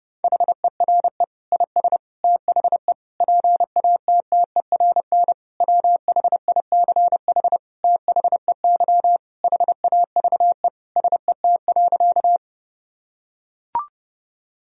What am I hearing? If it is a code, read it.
Morse code, heard as a keyed tone, about 30 wpm, 700 Hz. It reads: HERE IS THE PATTERN WHICH THEY HAVE SET.
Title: Sentences from Top 1000 Words - Review Part 2 - 30wpm